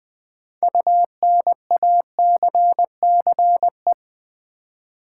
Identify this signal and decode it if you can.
Morse code, heard as a keyed tone, about 20 wpm, 700 Hz. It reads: UNACCE
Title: Build A Word - 6 Letters At A Time - 12-Letter Words - 20wpm